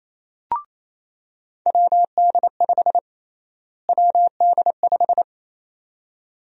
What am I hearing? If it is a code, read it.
Morse code, heard as a keyed tone, about 28 wpm, 700 Hz. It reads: WD5 WD5
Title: USA Call Sign Prefixes - Repeated 2x - 28wpm